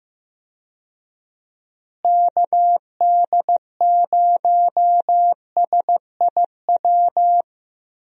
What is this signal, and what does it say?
Morse code, heard as a keyed tone, about 15 wpm, 700 Hz. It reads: KD0SIW
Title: USA Call Signs - 6 Characters - Repeated 2x - 15wpm